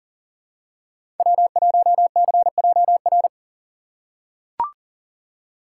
Morse code, heard as a keyed tone, about 40 wpm, 700 Hz. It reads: W1CJR
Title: USA Call Signs - 3x Character Spacing - 40wpm